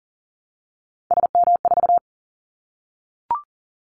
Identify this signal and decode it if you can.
Morse code, heard as a keyed tone, about 40 wpm, 700 Hz. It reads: SM4
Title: Callsign Prefixes for Top 25 DXCC - Repeated 3x - 40wpm